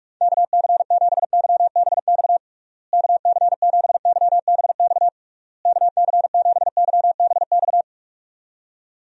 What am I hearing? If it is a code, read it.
Morse code, heard as a keyed tone, about 47 wpm, 700 Hz. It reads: KC7YBX KC7YBX KC7YBX
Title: USA Call Signs - 6 Characters - Repeated 3x - 45wpm